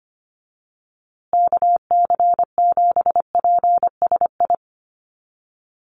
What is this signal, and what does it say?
Morse code, heard as a keyed tone, about 25 wpm, 700 Hz. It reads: KC7PSI
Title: USA Call Signs Repeated 3x 25wpm